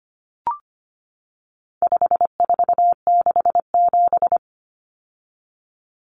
Morse code, heard as a keyed tone, about 25 wpm, 700 Hz. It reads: 5467
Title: Common Cut Numbers - 4-Digit 25wpm